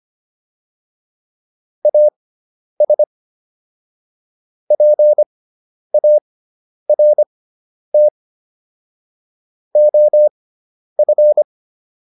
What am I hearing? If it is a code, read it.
Morse code, heard as a keyed tone, about 25 wpm, spaced as for 10 wpm, 600 Hz. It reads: AS PART OF